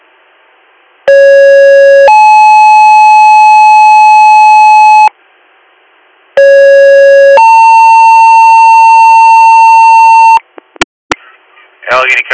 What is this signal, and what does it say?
Two-tone page, heard as a one-second tone, then a three-second tone, about 550 then 860 Hz, about 550 then 900 Hz, with speech. It is heard from 1.1 to 10.4 s.